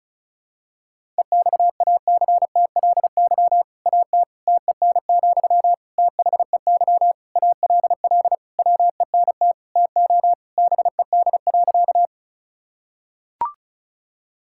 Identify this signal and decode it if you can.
Morse code, heard as a keyed tone, about 36 wpm, 700 Hz. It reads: EXACTLY AT TEN, THEY ALL WENT TO BED.
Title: Sentences from Top 1000 Words - Review Part 6 - 35wpm